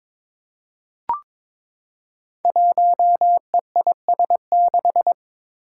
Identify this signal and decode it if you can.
Morse code, heard as a keyed tone, about 22 wpm, 700 Hz. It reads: 1EIS6
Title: Five Challenging Letter-Number Combinations 22wpm